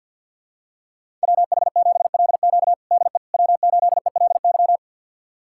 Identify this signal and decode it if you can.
Morse code, heard as a keyed tone, about 50 wpm, 700 Hz. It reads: WH7LQ DE W8ELY
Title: USA Callsign DE Callsign - 5 Characters - 50wpm